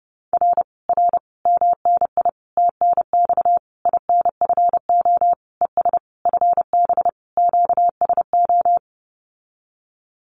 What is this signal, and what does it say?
Morse code, heard as a keyed tone, about 32 wpm, 700 Hz. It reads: R R MNI TNX INFO ES FB QSO
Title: Common QSO Phrases Encore 30wpm